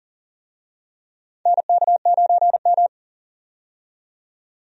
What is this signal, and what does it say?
Morse code, heard as a keyed tone, about 40 wpm, 690 Hz. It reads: NK9M
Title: USA Call Signs - 4 Characters - 40wpm